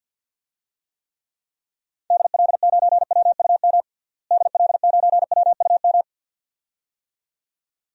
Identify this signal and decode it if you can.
Morse code, heard as a keyed tone, about 50 wpm, 700 Hz. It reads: DL9WUM DL9WUM
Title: DXCC-100 Call Signs - 6 Characters - Repeated 2x - 50wpm